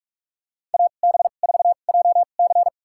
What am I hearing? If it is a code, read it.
Morse code, heard as a keyed tone, about 46 wpm, spaced as for 33 wpm, 700 Hz. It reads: AB4JC
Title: USA Call Signs - 2x Character Spacing - 45wpm